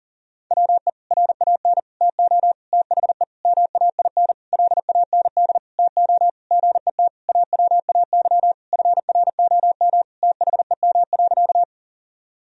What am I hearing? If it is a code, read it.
Morse code, heard as a keyed tone, about 37 wpm, 700 Hz. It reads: WE RAN TO THE MAIN LAND TO GET AWAY FROM THEM.
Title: Sentences from Top 1000 Words - Review Part 2 - 40wpm